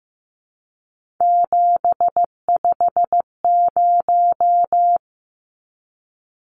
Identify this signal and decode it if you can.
Morse code, heard as a keyed tone, about 15 wpm, 700 Hz. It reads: 750
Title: Numbers - 3-Digit 15wpm